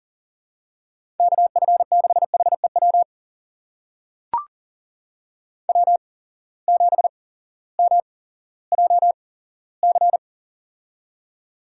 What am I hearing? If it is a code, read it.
Morse code, heard as a keyed tone, about 40 wpm, spaced as for 18 wpm, 700 Hz. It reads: KF6HEW W7MJC
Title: USA Call Signs - 8x Character Spacing - 40wpm